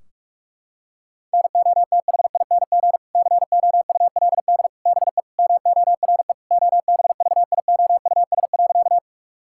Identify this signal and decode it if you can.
Morse code, heard as a keyed tone, about 41 wpm, 700 Hz. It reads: NOTHING COULD BE MORE OBVIOUS.
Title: Sentences from Top 1600 Words 45wpm